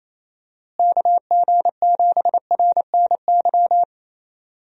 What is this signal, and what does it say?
Morse code, heard as a keyed tone, about 27 wpm, 700 Hz. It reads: KG7RNY
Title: USA Call Signs - 6 Characters - Repeated 3x - 28wpm